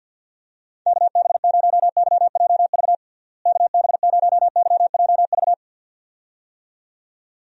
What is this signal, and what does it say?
Morse code, heard as a keyed tone, about 51 wpm, 700 Hz. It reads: KB0YJV KB0YJV
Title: USA Call Signs - 6 Characters - Repeated 2x - 50wpm